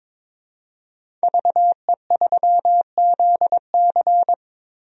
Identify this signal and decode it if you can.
Morse code, heard as a keyed tone, about 22 wpm, 700 Hz. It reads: VE3ZC
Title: DXCC-100 Call Signs - 5 Characters - 22wpm